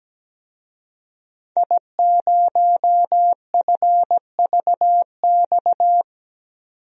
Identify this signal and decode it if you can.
Morse code, heard as a keyed tone, about 17 wpm, 700 Hz. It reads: I0FVX
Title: DXCC-100 Call Signs - 5 Characters - 17wpm